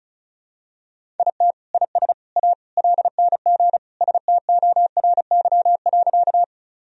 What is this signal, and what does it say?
Morse code, heard as a keyed tone, about 36 wpm, 700 Hz. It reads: IT IS A LNG STORY.
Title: Sentences from Top 300 Words and Abbreviations 35wpm